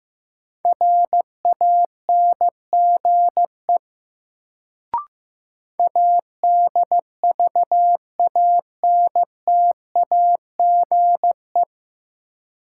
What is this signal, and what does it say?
Morse code, heard as a keyed tone, about 15 wpm, 700 Hz. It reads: RANGE ADVANTAGE
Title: Sentences from Top 900 Words - New Word Drill - 15wpm